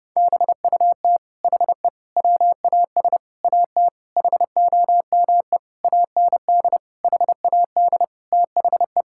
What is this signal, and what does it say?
Morse code, heard as a keyed tone, about 30 wpm, 700 Hz. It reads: BUT HE WAS AT HOME AND HAD THE
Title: Sentences from Top 1000 Words - Review Part 5 - 30wpm